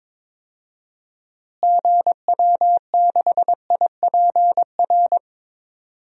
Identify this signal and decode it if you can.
Morse code, heard as a keyed tone, about 22 wpm, 700 Hz. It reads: GW6IPR
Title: DXCC-100 Call Signs - 6 Characters - Repeated 3x - 22wpm